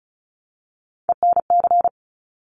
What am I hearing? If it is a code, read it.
Morse code, heard as a keyed tone, about 36 wpm, 700 Hz. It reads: ENC
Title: Top Three Letter Combinations in Words 35wpm